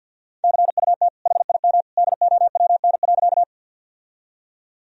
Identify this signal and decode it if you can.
Morse code, heard as a keyed tone, about 46 wpm, 700 Hz. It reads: CUT HIM DOWN.
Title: Sentences from Top 1000 Words - Review Part 1 - 50wpm